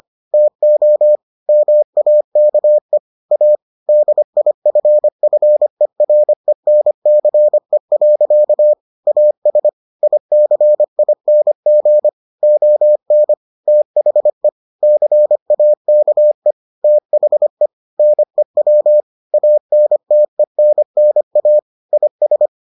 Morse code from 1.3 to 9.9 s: MAKE A DIFFERENCE. AS

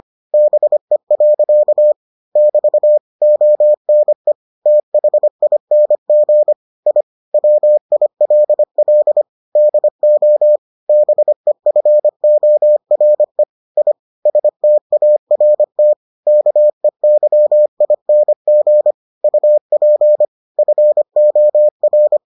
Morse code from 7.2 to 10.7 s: WILL DO

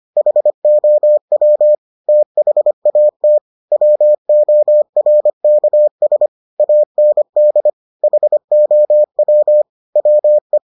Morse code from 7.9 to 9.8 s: HOW

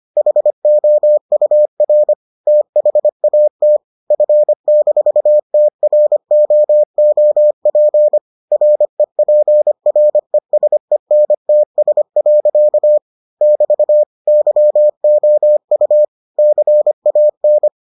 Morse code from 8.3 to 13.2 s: REPRESENTS.

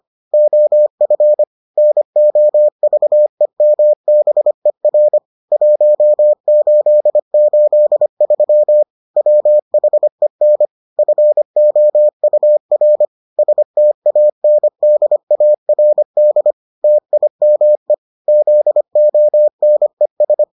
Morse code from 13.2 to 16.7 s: STANDARD

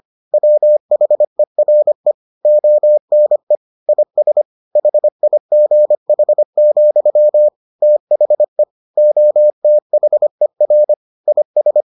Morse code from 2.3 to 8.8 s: ONE IS HIGH, THE

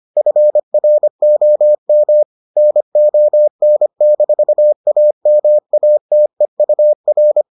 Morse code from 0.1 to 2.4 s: FROM

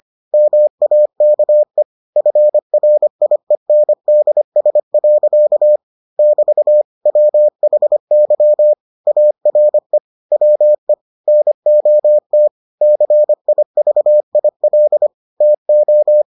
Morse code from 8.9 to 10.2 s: ARE

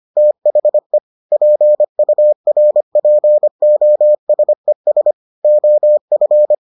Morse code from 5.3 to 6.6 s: OF

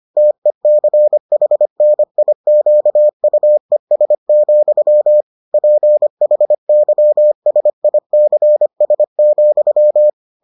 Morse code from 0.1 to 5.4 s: TECHNIQUES,

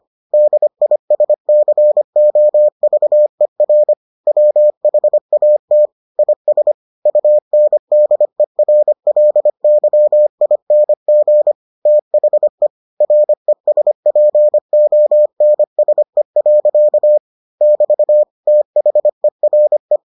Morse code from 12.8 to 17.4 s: RESPONSE.

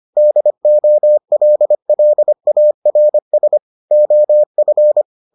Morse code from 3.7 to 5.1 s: OF

